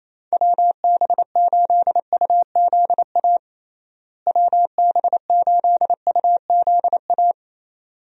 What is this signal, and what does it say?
Morse code, heard as a keyed tone, about 27 wpm, 700 Hz. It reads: WB8UZA WB8UZA
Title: USA Call Signs - 6 Characters - Repeated 2x - 28wpm